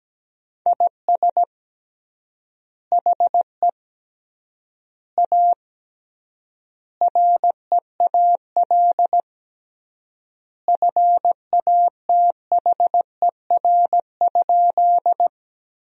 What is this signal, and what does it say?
Morse code, heard as a keyed tone, about 17 wpm, 700 Hz. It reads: IS HE A REAL FATHER?
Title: 3x Word-Spacing - Sentences from Top 300 Words - 17wpm